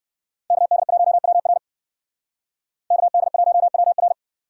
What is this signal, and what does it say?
Morse code, heard as a keyed tone, about 57 wpm, 690 Hz. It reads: KD1WR KD1WR